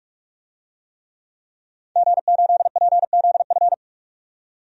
Morse code, heard as a keyed tone, about 41 wpm, 700 Hz. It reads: G8PZF
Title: DXCC-100 Call Signs - 5 Characters - 45wpm